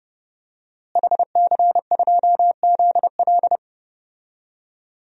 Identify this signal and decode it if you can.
Morse code, heard as a keyed tone, about 30 wpm, 700 Hz. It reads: HC2ZL
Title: DXCC-100 Call Signs - 5 Characters - Repeated 2x - 30wpm